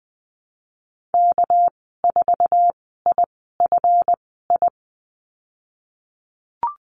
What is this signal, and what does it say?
Morse code, heard as a keyed tone, about 20 wpm, spaced as for 15 wpm, 700 Hz. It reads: K4IFI
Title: 245 Lesson 39g - Letter z - Call Signs 20wpm